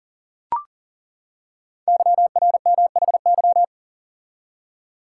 Morse code, heard as a keyed tone, about 40 wpm, 700 Hz. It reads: YRMHY
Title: Code Groups - 5 Letters - 40wpm